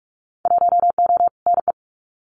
Morse code, heard as a keyed tone, about 46 wpm, 700 Hz. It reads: 1O NE
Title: Winter Field Day Exchanges 45wpm